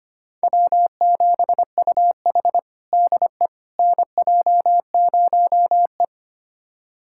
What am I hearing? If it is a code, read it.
Morse code, heard as a keyed tone, about 25 wpm, 700 Hz. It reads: W7UH DE NJ0E
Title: USA Callsign DE Callsign - 4 Characters - 25wpm